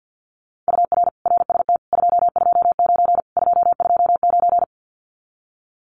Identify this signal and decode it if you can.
Morse code, heard as a keyed tone, about 50 wpm, 700 Hz. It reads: UR RST 229 229
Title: Common QSO Phrases Encore 50wpm